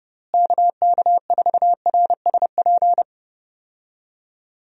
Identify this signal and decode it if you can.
Morse code, heard as a keyed tone, about 30 wpm, 700 Hz. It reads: KK4RSP
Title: USA Call Signs - 6 Characters - Repeated 2x - 30wpm